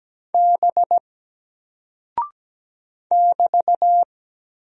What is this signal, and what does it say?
Morse code, heard as a keyed tone, about 17 wpm, 700 Hz. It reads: B <BT>
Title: Single Character (Letter:Number:Punctuation:Prosign) 17wpm